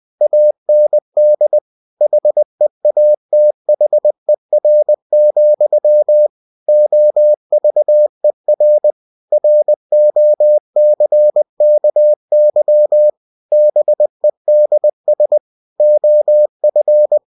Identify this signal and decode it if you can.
Morse code, heard as a keyed tone, about 20 wpm, 600 Hz. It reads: AND HEATHER, OVER ROCKY BEDS OF
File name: cw-books / winnie-the-pooh / winnie-20wpm-page-0028